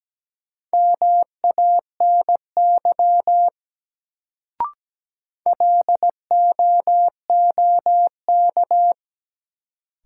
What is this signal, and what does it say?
Morse code, heard as a keyed tone, about 17 wpm, 700 Hz. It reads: MANY LOOK
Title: Sentences from Top 100 Words - New Word Drill - 17wpm